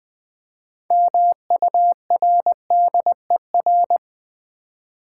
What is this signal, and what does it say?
Morse code, heard as a keyed tone, about 20 wpm, 700 Hz. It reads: MURDER
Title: Build A Word - 2 Letters At A Time - 6-Letter Words - 20wpm